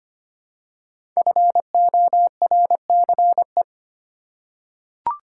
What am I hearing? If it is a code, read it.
Morse code, heard as a keyed tone, about 25 wpm, 700 Hz. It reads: FORCE
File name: 5 Letter Words - First 3 Letters Given - 25wpm